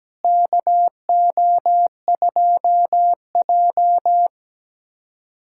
Morse code, heard as a keyed tone, about 17 wpm, 700 Hz. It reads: KO2J
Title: USA Callsign DE Callsign - 4 Characters - 17wpm